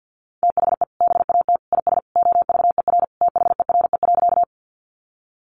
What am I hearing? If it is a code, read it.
Morse code, heard as a keyed tone, about 50 wpm, 700 Hz. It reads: THE BAT IS OVER THERE.
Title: Sentences from Top 1500 Words 50wpm